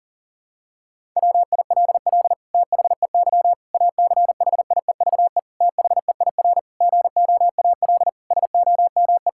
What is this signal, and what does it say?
Morse code, heard as a keyed tone, about 40 wpm, 700 Hz. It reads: WILL THEY ACHIEVE THEIR GOAL SOME
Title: Sentences from Top 1000 Words - Review Part 6 - 40wpm